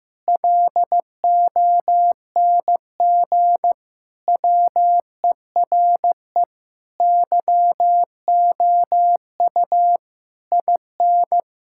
Morse code, heard as a keyed tone, about 16 wpm, 700 Hz. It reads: LONG WERE YOU IN
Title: Sentences from Top 1600 Words 15wpm